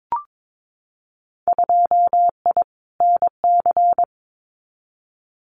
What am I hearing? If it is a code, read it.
Morse code, heard as a keyed tone, about 22 wpm, 700 Hz. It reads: 2I NC
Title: Winter Field Day Exchanges 22wpm